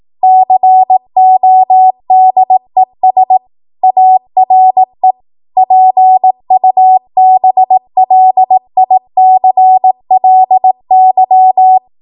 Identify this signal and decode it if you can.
Morse code, heard as a keyed tone, about 18 wpm, 750 Hz. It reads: CODES ARE PUBLICLY
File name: Morse / 18WPM